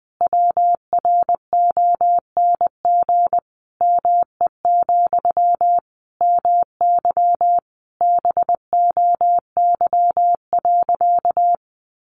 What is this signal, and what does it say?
Morse code, heard as a keyed tone, about 20 wpm, 700 Hz. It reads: WRONG ME, MY BOY.
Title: Sentences from Top 900 Words 20wpm